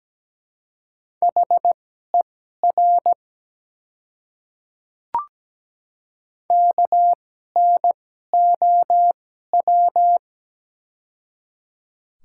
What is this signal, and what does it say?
Morse code, heard as a keyed tone, about 17 wpm, spaced as for 12 wpm, 700 Hz. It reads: HER KNOW
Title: 2x Character Spacing - Top 100 Words - 17wpm